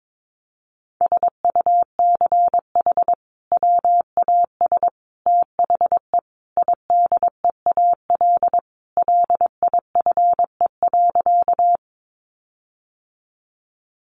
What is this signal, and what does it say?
Morse code, heard as a keyed tone, about 22 wpm, 700 Hz. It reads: SUCH WAS THE IDEAL LIFE.
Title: Sentences from Top 1400 Words 22wpm